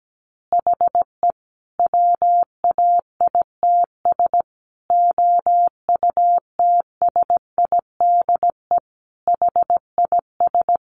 Morse code, heard as a keyed tone, about 17 wpm, 700 Hz. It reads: HE WAITS OUTSIDE HIS
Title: Additional Easy Phrases 17wpm